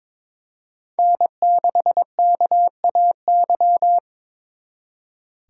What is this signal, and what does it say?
Morse code, heard as a keyed tone, about 26 wpm, 700 Hz. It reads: N6KAY
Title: Speed Racing - USA Call Signs 22wpm